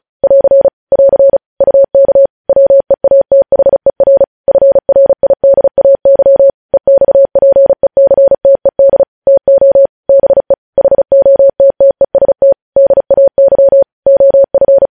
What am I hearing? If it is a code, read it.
Morse code, heard as a keyed tone, about 36 wpm, 550 Hz. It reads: <AR> <AR> UK WEATHER FRIDAY EXPECTED TO BE HOTTEST DAY OF
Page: news headlines in morse code at 35 wpm